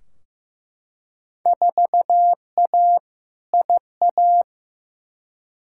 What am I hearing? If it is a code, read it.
Morse code, heard as a keyed tone, about 15 wpm, 700 Hz. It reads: 4A IA
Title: Field Day Exchanges - 15wpm